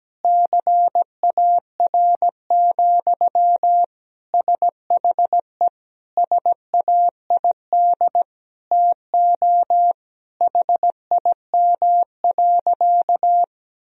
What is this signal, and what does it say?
Morse code, heard as a keyed tone, about 17 wpm, 700 Hz. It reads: CAR, SHE SAID TO HIM.
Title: Sentences from Top 1000 Words - Review Part 1 - 17wpm